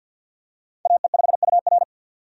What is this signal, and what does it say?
Morse code, heard as a keyed tone, about 50 wpm, 700 Hz. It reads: AE5UR
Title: USA Call Signs - 5 Characters - 50wpm